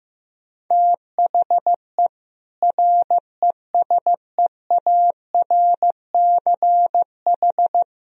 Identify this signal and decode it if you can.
Morse code, heard as a keyed tone, about 15 wpm, 700 Hz. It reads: THE RESEARCH
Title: Sentences from Top 1000 Words - Review Part 4 - 15wpm